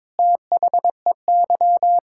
Morse code, heard as a keyed tone, about 22 wpm, 700 Hz. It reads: THEY
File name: practice / 22wpm / Sentences with 8-Word Phrases - 22wpm